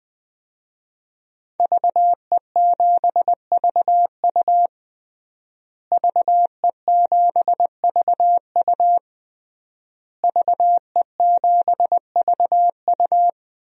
Morse code, heard as a keyed tone, about 20 wpm, 700 Hz. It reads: VE7VU VE7VU VE7VU
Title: DXCC-100 Call Signs - 5 Characters - Repeated 3x - 20wpm